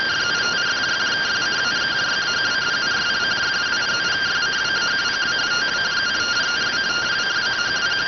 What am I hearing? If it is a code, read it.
Two-tone teleprinter signal, about 170 Hz shift.